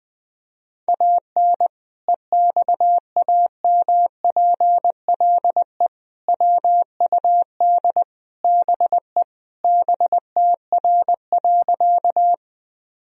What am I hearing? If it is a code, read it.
Morse code, heard as a keyed tone, about 20 wpm, 700 Hz. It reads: AN EXAMPLE WUD BE BTR.
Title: Sentences from Top 400 Words and Abbreviations 20wpm